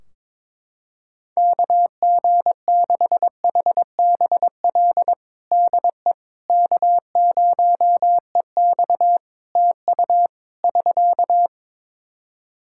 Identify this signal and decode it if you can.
Morse code, heard as a keyed tone, about 22 wpm, 700 Hz. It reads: KG6HBL DE K0EX TU <SK>
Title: K1USN SST QSO Practice 22wpm